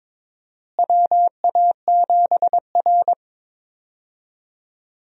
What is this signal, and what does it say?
Morse code, heard as a keyed tone, about 22 wpm, 700 Hz. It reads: WA7R
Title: USA Call Signs - 4 Characters - Repeated 2x - 22wpm